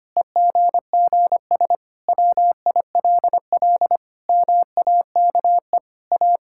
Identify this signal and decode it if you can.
Morse code, heard as a keyed tone, about 25 wpm, 700 Hz. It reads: EGGS WILL MAKE A
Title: Pangram Challenge v2 - 25wpm